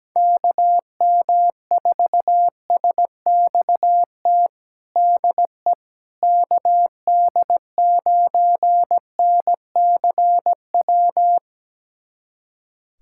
Morse code, heard as a keyed tone, about 17 wpm, 700 Hz. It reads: KM4SXT DE KD9NCW
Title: USA Callsign DE Callsign - 6 Characters - 17wpm